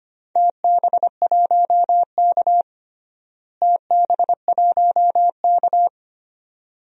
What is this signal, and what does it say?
Morse code, heard as a keyed tone, about 25 wpm, 700 Hz. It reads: TB1K TB1K
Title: DXCC-100 Call Signs - 4 Characters - Repeated 2x - 25wpm